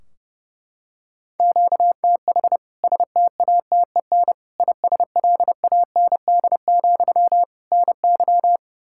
Morse code, heard as a keyed tone, about 32 wpm, 700 Hz. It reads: QTH STATEN ISLAND, NY
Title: K1USN SST QSO Practice 30wpm